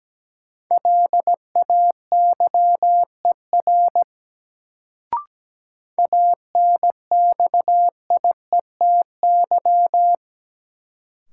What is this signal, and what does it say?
Morse code, heard as a keyed tone, about 17 wpm, 700 Hz. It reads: LAYER ANXIETY
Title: Sentences from Top 1600 Words - New Word Drill - 17wpm